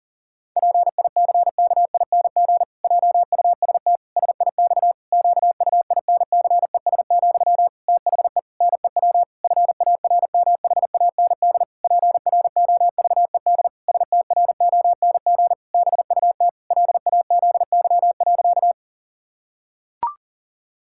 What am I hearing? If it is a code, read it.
Morse code, heard as a keyed tone, about 40 wpm, 700 Hz. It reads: PICKING JUST SIX QUINCES, THE NEW FARMHAND PROVED STRONG BUT LAZY.
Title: Pangram Challenge v2 - 40wpm